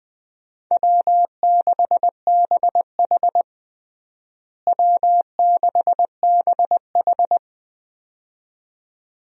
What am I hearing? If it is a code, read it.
Morse code, heard as a keyed tone, about 21 wpm, 700 Hz. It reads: W6BH W6BH
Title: USA Call Signs - 4 Characters - Repeated 2x - 20wpm